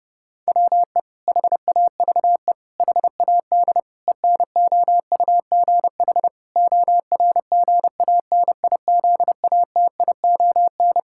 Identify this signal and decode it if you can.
Morse code, heard as a keyed tone, about 30 wpm, 700 Hz. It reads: WE HAVE HAD ENOUGH ORGANIZATION
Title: Sentences from Top 1000 Words - Review Part 5 - 30wpm